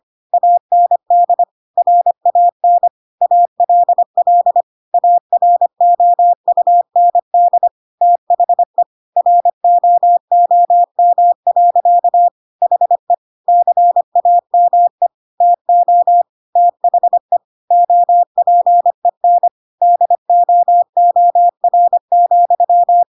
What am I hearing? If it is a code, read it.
Morse code, heard as a keyed tone, about 25 wpm, 700 Hz. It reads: AND RAN ALL AROUND THE ROOM. HE CAME TO THE OPEN DOOR,
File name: Pinocchio / Pinocchio ch3 25 25 ik1hge